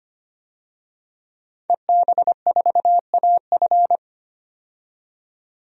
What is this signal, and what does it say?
Morse code, heard as a keyed tone, about 25 wpm, 700 Hz. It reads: EB4AF
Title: DXCC-100 Call Signs - 5 Characters - 25wpm